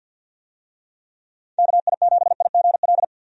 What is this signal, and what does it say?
Morse code, heard as a keyed tone, about 51 wpm, 700 Hz. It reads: KI7IGL
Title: USA Call Signs - 6 Characters - 50wpm